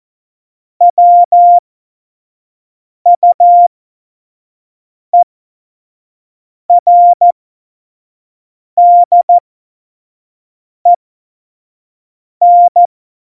Morse code, heard as a keyed tone, about 14 wpm, spaced as for 5 wpm, 700 Hz. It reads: WUERDEN
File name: Morsekurs / Witze / Witz 08. August 17, Tempo 25 F70